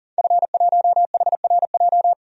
Morse code, heard as a keyed tone, about 40 wpm, 700 Hz. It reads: F1HRJ
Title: DXCC-100 Call Signs - 5 Characters - 40wpm